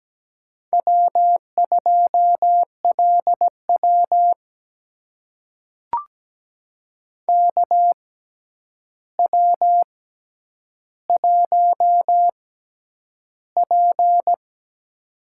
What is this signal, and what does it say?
Morse code, heard as a keyed tone, about 17 wpm, spaced as for 9 wpm, 700 Hz. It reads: W2LW KW1P
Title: USA Call Signs - 4 Characters - 6x Character Spacing - 17wpm